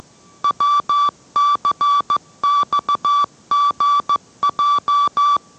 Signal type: Morse code